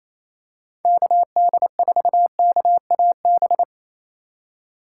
Morse code, heard as a keyed tone, about 28 wpm, 700 Hz. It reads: KD4KAB
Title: USA Call Signs - 4x Character Spacing - 28wpm